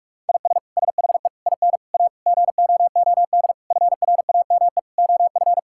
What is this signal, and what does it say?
Morse code, heard as a keyed tone, about 41 wpm, 700 Hz. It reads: IS SHE IN A GOOD FRAME OF
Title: Sentences from Top 1000 Words - Review Part 6 - 45wpm